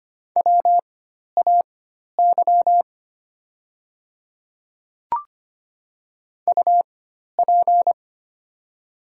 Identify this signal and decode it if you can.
Morse code, heard as a keyed tone, about 25 wpm, spaced as for 12 wpm, 700 Hz. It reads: WAY UP
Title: 4x Character Spacing - Top 100 Words - 25wpm